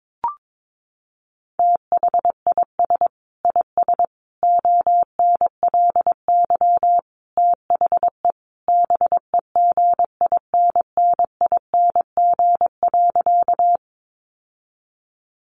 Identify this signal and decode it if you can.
Morse code, heard as a keyed tone, about 22 wpm, 700 Hz. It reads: THIS IS ONLY THE BEGINNING.
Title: Sentences from Top 800 Words 22wpm